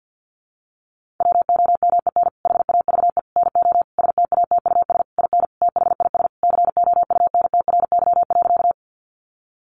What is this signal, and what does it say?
Morse code, heard as a keyed tone, about 50 wpm, 700 Hz. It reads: WOMEN HAVE NO STATUS IN THIS COUNTRY.
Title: Sentences from Top 1400 Words 50wpm